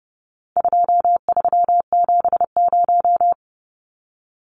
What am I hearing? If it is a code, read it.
Morse code, heard as a keyed tone, about 30 wpm, 700 Hz. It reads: 2370
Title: Numbers - 4-Digit 30wpm